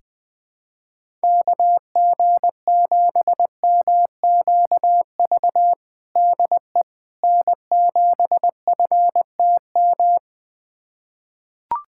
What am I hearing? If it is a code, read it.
Morse code, heard as a keyed tone, about 21 wpm, 700 Hz. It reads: KG7MQV DE N7FTM
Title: USA Callsign DE Callsign 20wpm